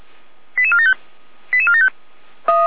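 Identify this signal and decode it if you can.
Five-tone selective call (ZVEI), heard as quick runs of five tones, tones of about 70 ms. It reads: 88475 88475
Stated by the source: ZVEI five-tone selective call received by SDR, from the audio server of Feuerwehr Orlamünde fire brigade.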